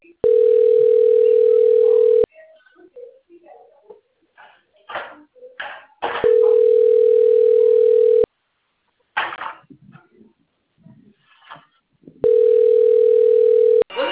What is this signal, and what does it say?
Ringback tone (North America), 440 + 480 Hz.
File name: out- - - - - .64681